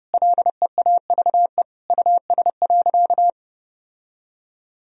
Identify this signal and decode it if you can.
Morse code, heard as a keyed tone, about 30 wpm, 700 Hz. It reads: LEAVE US.
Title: Sentences from Top 1000 Words - Review Part 3 - 30wpm